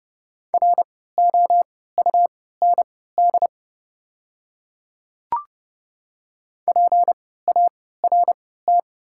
Morse code, heard as a keyed tone, about 30 wpm, spaced as for 17 wpm, 690 Hz. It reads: ROUND PART
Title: 3x Character Spacing - Top 200 Words - 30wpm